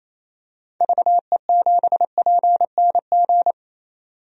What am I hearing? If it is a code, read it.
Morse code, heard as a keyed tone, about 27 wpm, 700 Hz. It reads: VE7PNG
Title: DXCC-100 Call Signs - 6 Characters - 28wpm